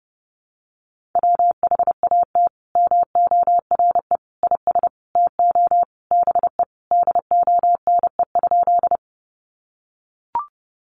Morse code, heard as a keyed tone, about 28 wpm, 700 Hz. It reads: WHAT MORE IS TO BE DONE?
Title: Sentences from Top 500 Words 30wpm